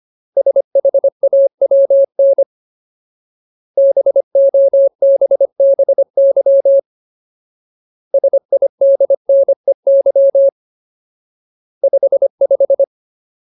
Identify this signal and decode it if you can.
Morse code, heard as a keyed tone, about 25 wpm, 550 Hz. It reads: SHAWN BOBBY SIDNEY 55
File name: Ultimate Word List / 25WPM Ultimate